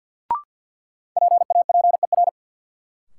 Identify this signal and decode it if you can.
Morse code, heard as a keyed tone, about 53 wpm, 700 Hz. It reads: PAPER
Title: Sentences from Top 300 Words - New Word Drill - 50wpm